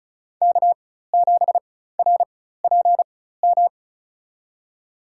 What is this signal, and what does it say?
Morse code, heard as a keyed tone, about 36 wpm, spaced as for 16 wpm, 700 Hz. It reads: K7RPM
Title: USA Call Signs - 4x Character Spacing - 35wpm